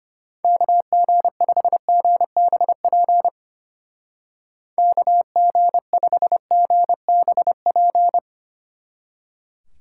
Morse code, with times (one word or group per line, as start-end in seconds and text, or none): 0.4-3.3 KG5GBP
4.8-8.2 KG5GBP